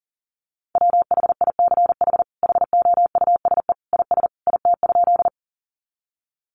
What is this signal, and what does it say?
Morse code, heard as a keyed tone, about 40 wpm, 700 Hz. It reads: WHICH HOUSE IS IT?
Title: Sentences from Top 200 Words 40wpm